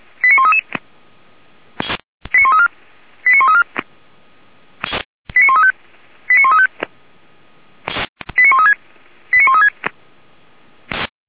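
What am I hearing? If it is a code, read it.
Five-tone selective call (ZVEI), heard as quick runs of five tones, tones of about 70 ms. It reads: 89133 89134 89134 89135 89135 89136 89136
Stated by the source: ZVEI five-tone call received by SDR, from feuerwehr-orlamuende.org archive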